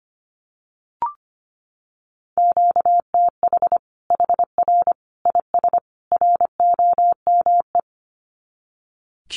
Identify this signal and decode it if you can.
Morse code, heard as a keyed tone, about 25 wpm, 700 Hz. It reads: QTH HR IS ROME
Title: Common QSO Phrases 25wpm